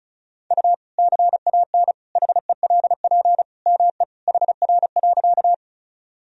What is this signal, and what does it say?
Morse code, heard as a keyed tone, about 36 wpm, 700 Hz. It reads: U CAN HELP ME HR.
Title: Sentences from Top 200 Words and Abbreviations 35wpm